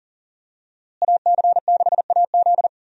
Morse code, heard as a keyed tone, about 38 wpm, 700 Hz. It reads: AC6AZ
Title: USA Call Signs - 5 Characters - Repeated 3x - 40wpm